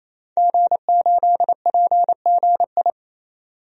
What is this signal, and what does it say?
Morse code, heard as a keyed tone, about 28 wpm, 700 Hz. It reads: G8PGI
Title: DXCC-100 Call Signs - 5 Characters - 28wpm